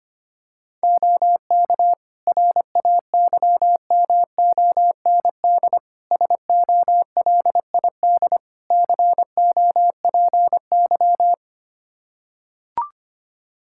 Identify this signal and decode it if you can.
Morse code, heard as a keyed tone, about 25 wpm, 700 Hz. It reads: OK RAYMOND SOLID COPY
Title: Practice QSO - Exchange 3 - More Info - 25wpm